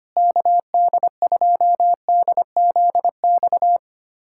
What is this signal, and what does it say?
Morse code, heard as a keyed tone, about 25 wpm, 700 Hz. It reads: KD2DZX
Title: USA Callsign DE Callsign - 6 Characters - 25wpm